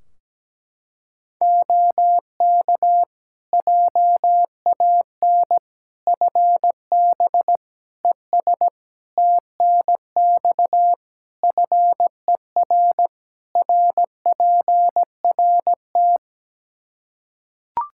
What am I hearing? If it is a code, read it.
Morse code, heard as a keyed tone, about 17 wpm, 700 Hz. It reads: OK JAN FB ES TNX FER RPRT
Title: K1USN SST QSO Practice 17wpm